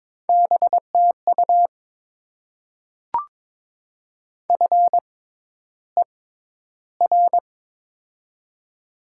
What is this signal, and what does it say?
Morse code, heard as a keyed tone, about 22 wpm, spaced as for 11 wpm, 700 Hz. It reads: BTU FER
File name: Top 20 QSO Elements - 6x Character Spacing - 22wpm